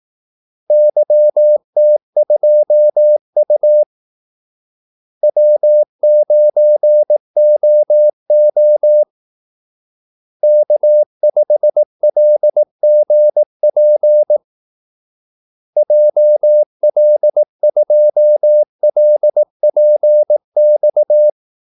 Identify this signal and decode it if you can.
Morse code, heard as a keyed tone, about 18 wpm, 600 Hz. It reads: YT2U W9OO K5LGP JL2LPX
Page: Kaldesignaler 18 wpm